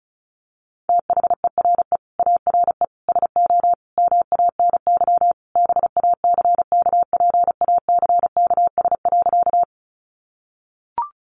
Morse code, heard as a keyed tone, about 34 wpm, 700 Hz. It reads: THERE ARE SO MANY BACKPACKS.
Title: Additional Easy Phrases 35wpm